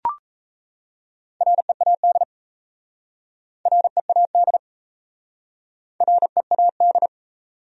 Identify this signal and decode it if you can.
Morse code, heard as a keyed tone, about 38 wpm, 700 Hz. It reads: READ READ READ